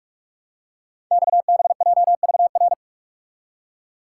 Morse code, heard as a keyed tone, about 44 wpm, 700 Hz. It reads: XBJVR
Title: Code Groups - 5 Letters - 45wpm